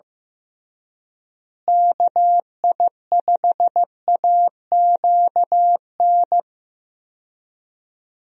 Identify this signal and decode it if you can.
Morse code, heard as a keyed tone, about 15 wpm, 700 Hz. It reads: KI5AQN
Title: USA Call Signs - 6 Characters - Repeated 2x - 15wpm